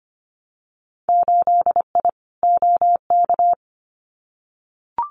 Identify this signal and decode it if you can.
Morse code, heard as a keyed tone, about 25 wpm, 700 Hz. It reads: 8I OK
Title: Winter Field Day Exchanges Encore 25wpm